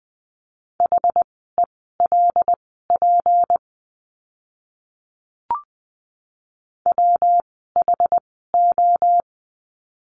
Morse code, heard as a keyed tone, about 20 wpm, spaced as for 14 wpm, 700 Hz. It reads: HELP WHO
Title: 192 Lesson 31f - Letter v - Words 20wpm